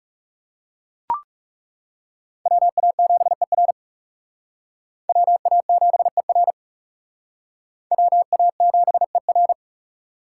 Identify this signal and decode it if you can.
Morse code, heard as a keyed tone, about 40 wpm, 700 Hz. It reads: WA7ER WA7ER WA7ER